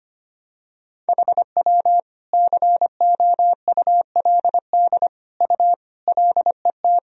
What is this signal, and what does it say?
Morse code, heard as a keyed tone, about 25 wpm, 700 Hz. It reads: HW COULD U LET
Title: Sentences from Top 300 Words and Abbreviations 25wpm